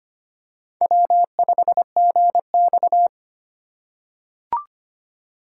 Morse code, heard as a keyed tone, about 25 wpm, 700 Hz. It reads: W5GX
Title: USA Call Signs - 4 Characters - 4x Character Spacing - 25wpm